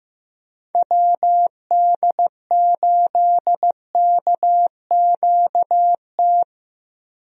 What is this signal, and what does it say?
Morse code, heard as a keyed tone, about 15 wpm, 700 Hz. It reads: WD8KQT
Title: USA Call Signs - 6 Characters - Repeated 3x - 15wpm